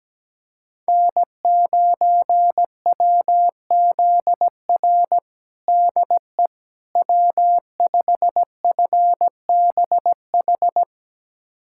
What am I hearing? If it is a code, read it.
Morse code, heard as a keyed tone, about 17 wpm, 700 Hz. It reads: N9WZR DE W5FBH
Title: USA Callsign DE Callsign - 5 Characters - 17wpm